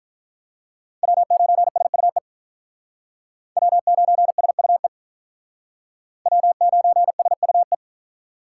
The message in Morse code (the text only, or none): W9SUE W9SUE W9SUE